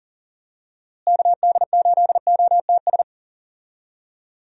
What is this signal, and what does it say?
Morse code, heard as a keyed tone, about 40 wpm, 700 Hz. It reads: KD8OTS
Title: USA Call Signs - 6 Characters - Repeated 3x - 40wpm